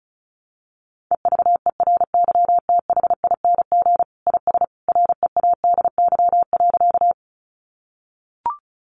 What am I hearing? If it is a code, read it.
Morse code, heard as a keyed tone, about 34 wpm, 700 Hz. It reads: EVERYTHING IS READY.
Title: Sentences from Top 700 Words 35wpm